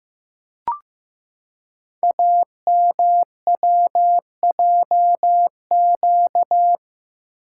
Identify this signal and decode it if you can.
Morse code, heard as a keyed tone, about 15 wpm, 700 Hz. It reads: AMWJQ
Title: Code Groups - 5 Letters - 15wpm